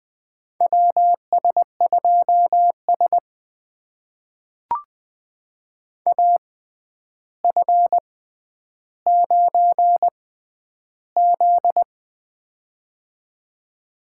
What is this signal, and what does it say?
Morse code, heard as a keyed tone, about 20 wpm, spaced as for 10 wpm, 700 Hz. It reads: WS2S AF9Z
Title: USA Call Signs - 4 Characters - 6x Character Spacing - 20wpm